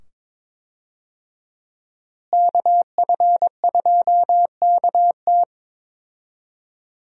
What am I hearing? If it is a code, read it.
Morse code, heard as a keyed tone, about 21 wpm, 700 Hz. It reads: KF2KT
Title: USA Call Signs - 5 Characters - Repeated 2x - 22wpm